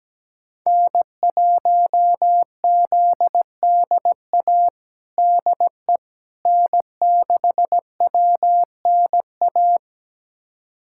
Morse code, heard as a keyed tone, about 17 wpm, 700 Hz. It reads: N1ZDA DE N6WNA
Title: USA Callsign DE Callsign - 5 Characters - 17wpm